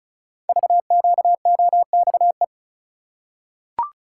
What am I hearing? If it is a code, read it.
Morse code, heard as a keyed tone, about 36 wpm, 700 Hz. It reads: VQOXE
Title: Code Groups - 5 Letters - 35wpm